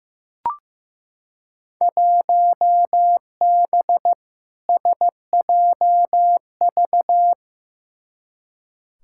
Morse code, heard as a keyed tone, about 15 wpm, 700 Hz. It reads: 1B SJV
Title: Field Day Exchanges Encore - 15wpm